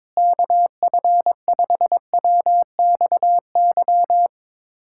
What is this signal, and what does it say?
Morse code, heard as a keyed tone, about 22 wpm, 700 Hz. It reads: KF5WXY
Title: USA Callsign DE Callsign 22wpm